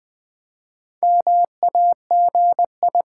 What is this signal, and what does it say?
Morse code, heard as a keyed tone, about 20 wpm, 700 Hz. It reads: MAGI
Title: Build A Word - 1 Letter At A Time - 5-Letter Words - 20wpm